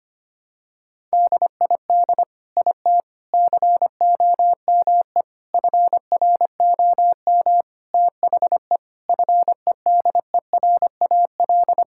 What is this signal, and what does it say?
Morse code, heard as a keyed tone, about 25 wpm, 700 Hz. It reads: DID IT COME FROM THE FEDERAL
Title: Sentences from Top 1000 Words - Review Part 5 - 25wpm